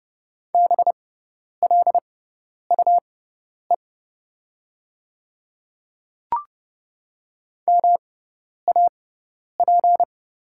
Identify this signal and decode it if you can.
Morse code, heard as a keyed tone, about 30 wpm, spaced as for 10 wpm, 700 Hz. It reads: BLUE MAP
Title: 6x Character Spacing - Top 500 Words - 30wpm